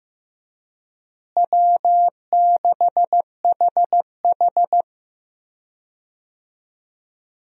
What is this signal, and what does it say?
Morse code, heard as a keyed tone, about 15 wpm, 700 Hz. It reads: W6HH
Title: USA Call Signs - 4 Characters - Repeated 2x - 15wpm